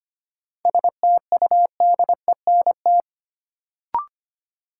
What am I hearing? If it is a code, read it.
Morse code, heard as a keyed tone, about 25 wpm, 700 Hz. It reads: STUDENT
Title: Sentences from Top 1000 Words - New Word Drill - 25wpm